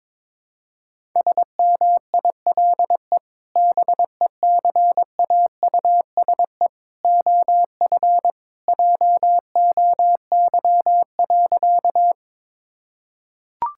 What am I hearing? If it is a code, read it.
Morse code, heard as a keyed tone, about 22 wpm, 700 Hz. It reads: SMILE BECAUSE OF JOY.
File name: practice / 22wpm / Sentences with 2-Word Phrases - 22wpm